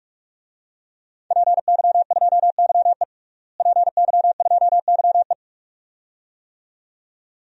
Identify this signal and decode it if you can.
Morse code, heard as a keyed tone, about 47 wpm, 700 Hz. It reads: PY2YE PY2YE
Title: DXCC-100 Call Signs - 5 Characters - Repeated 2x - 45wpm